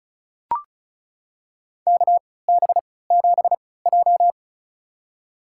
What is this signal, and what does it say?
Morse code, heard as a keyed tone, about 36 wpm, spaced as for 20 wpm, 700 Hz. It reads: KB7J
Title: USA Call Signs - 4 Characters - 3x Character Spacing - 35wpm